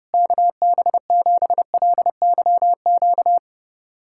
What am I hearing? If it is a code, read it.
Morse code, heard as a keyed tone, about 32 wpm, 700 Hz. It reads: KB7LYQ
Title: USA Callsign DE Callsign - 6 Characters - 30wpm